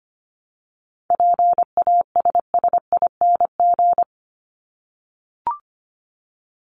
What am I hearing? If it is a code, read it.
Morse code, heard as a keyed tone, about 25 wpm, 700 Hz. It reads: PASSING
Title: ING Suffix - Top 100 Words with Base Words 25wpm